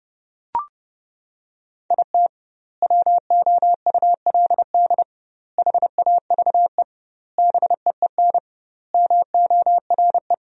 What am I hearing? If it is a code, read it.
Morse code, heard as a keyed tone, about 30 wpm, 700 Hz. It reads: IT WOULD HAVE BEEN MORE
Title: 2x Word-Spacing - Sentences from Top 400 Words - 30wpm